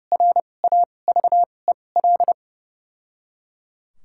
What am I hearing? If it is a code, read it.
Morse code, heard as a keyed tone, about 32 wpm, spaced as for 22 wpm, 700 Hz. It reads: RAVEL
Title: 2x Character Spacing - Top 500 Words - 30wpm